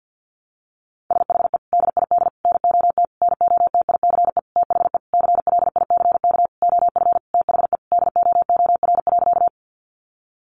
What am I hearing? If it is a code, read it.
Morse code, heard as a keyed tone, about 50 wpm, 700 Hz. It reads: SHE DID NOT NOTICE THE CLICK OF THE DOOR.
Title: Sentences from Top 1100 Words 50wpm